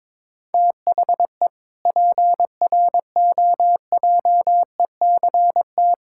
Morse code, heard as a keyed tone, about 22 wpm, 700 Hz. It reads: THE PROJECT
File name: Sentences with 7-Word Phrases - 22wpm